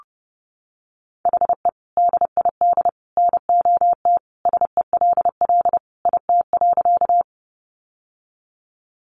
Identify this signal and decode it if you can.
Morse code, heard as a keyed tone, about 30 wpm, 700 Hz. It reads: HE DID NOT SELL IT.
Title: Sentences from Top 1100 Words 30wpm